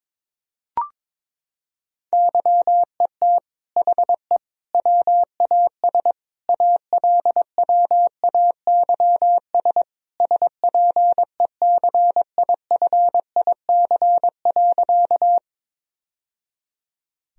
Morse code, heard as a keyed tone, about 22 wpm, 700 Hz. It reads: YET HE WAS ALWAYS SPECIFIC.
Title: Sentences from Top 1000 Words - Review Part 3 - 22wpm